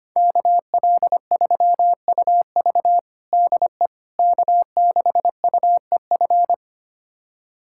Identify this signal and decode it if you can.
Morse code, heard as a keyed tone, about 25 wpm, 700 Hz. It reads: KL3UV DE K6UEF
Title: USA Callsign DE Callsign - 5 Characters - 25wpm